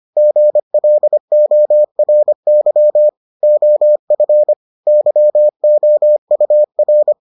Morse code, heard as a keyed tone, about 25 wpm, 600 Hz. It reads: GLORY OF YOUR